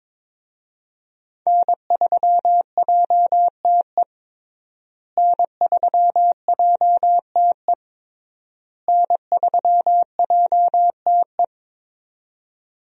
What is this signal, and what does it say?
Morse code, heard as a keyed tone, about 21 wpm, 700 Hz. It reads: N3JTE N3JTE N3JTE
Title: USA Call Signs - 5 Characters - Repeated 3x - 22wpm